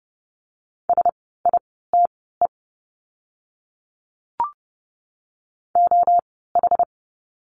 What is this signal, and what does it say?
Morse code, heard as a keyed tone, about 30 wpm, spaced as for 17 wpm, 700 Hz. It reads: SITE OH
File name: Lesson 16f - Letter c - Words 30wpm